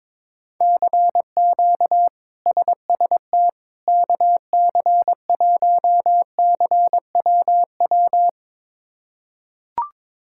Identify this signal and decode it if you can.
Morse code, heard as a keyed tone, about 22 wpm, 700 Hz. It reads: CQ SST KC1CWW
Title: K1USN SST QSO Practice - 22wpm